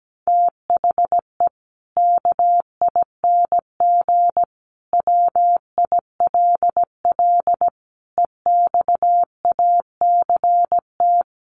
Morse code, heard as a keyed tone, about 17 wpm, 700 Hz. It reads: THE KING WILL EXACT
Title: Sentences from Top 1300 Words 17wpm